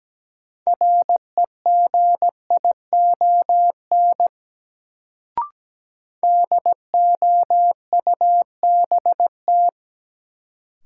Morse code, heard as a keyed tone, about 17 wpm, 700 Hz. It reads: REGION DOUBT
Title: Sentences from Top 1500 Words - New Word Drill - 17wpm